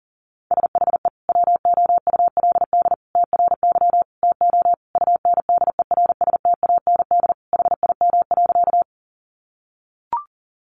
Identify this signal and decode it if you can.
Morse code, heard as a keyed tone, about 37 wpm, 700 Hz. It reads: SHE WOULD TRY TO UNDERSTAND HIM.
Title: Sentences from Top 600 Words 40wpm